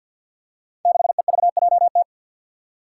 Morse code, heard as a keyed tone, about 50 wpm, 700 Hz. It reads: 6EVJT
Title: Five Challenging Letter-Number Combinations 50wpm